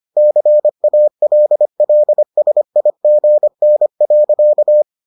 Morse code, heard as a keyed tone, about 25 wpm, 600 Hz. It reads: CALLSIGN.